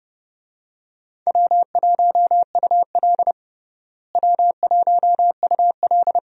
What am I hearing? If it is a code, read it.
Morse code, heard as a keyed tone, about 30 wpm, 700 Hz. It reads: W1UL W1UL
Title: USA Call Signs - 4 Characters - Repeated 2x - 30wpm